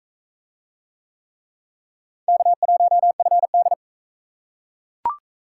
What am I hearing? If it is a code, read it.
Morse code, heard as a keyed tone, about 39 wpm, 700 Hz. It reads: K1FD